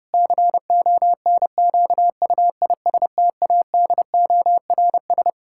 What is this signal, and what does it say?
Morse code, heard as a keyed tone, about 30 wpm, 700 Hz. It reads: CONQUISTADORS
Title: Pangram Challenge v2 - 30wpm